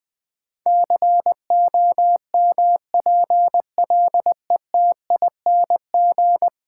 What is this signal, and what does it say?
Morse code, heard as a keyed tone, about 21 wpm, 700 Hz. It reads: COMPLETING
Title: Difficult Sentences with Hints - 20wpm